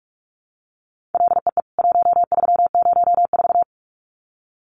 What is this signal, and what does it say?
Morse code, heard as a keyed tone, about 44 wpm, 700 Hz. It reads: LEE 1304
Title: CWT Exchange 45wpm